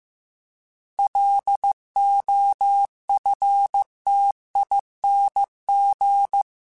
Morse code, heard as a keyed tone, about 15 wpm, 800 Hz. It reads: LOFTING
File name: Alphabetical / Alphabetical mp3s / Ls @13